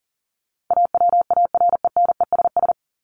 Morse code, heard as a keyed tone, about 40 wpm, 700 Hz. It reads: AWARENESS
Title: Top 1501-1750 Words 40wpm